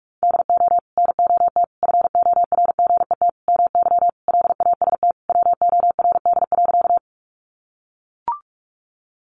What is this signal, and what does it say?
Morse code, heard as a keyed tone, about 44 wpm, 700 Hz. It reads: DO NOT FORGET MY LAST WORD.
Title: Sentences from Top 1200 Words 45wpm